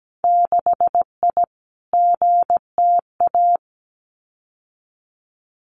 Morse code, heard as a keyed tone, about 17 wpm, 700 Hz. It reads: 6I GTA
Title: Winter Field Day Exchanges Encore 17wpm